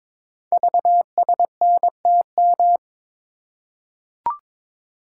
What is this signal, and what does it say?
Morse code, heard as a keyed tone, about 22 wpm, 700 Hz. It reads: VSNTM
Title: Code Groups - 5 Letter:Numbers - 22wpm